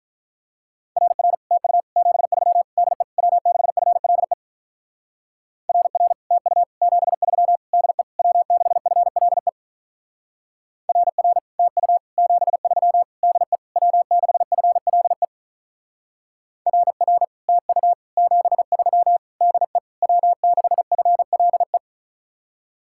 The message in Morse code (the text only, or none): RR TU 73 DE W6FLE RR TU 73 DE W6FLE RR TU 73 DE W6FLE RR TU 73 DE W6FLE